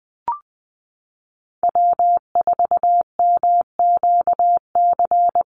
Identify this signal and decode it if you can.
Morse code, heard as a keyed tone, about 19 wpm, 700 Hz. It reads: W4MQC
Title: Call Signs - 20wpm